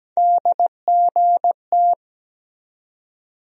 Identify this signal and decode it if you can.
Morse code, heard as a keyed tone, about 17 wpm, 700 Hz. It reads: DGT
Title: USA Call Sign Postfixes - 3 Letters 17wpm